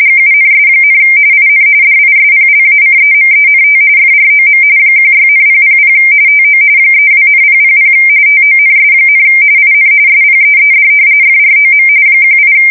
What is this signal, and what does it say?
RTTY, about 45.45 baud. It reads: GUNS, IN DISPLACEMENT FROM 16,000 TO 32,000 TONS, IN SPEED FROM 18
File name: fsk170 / battleshipnj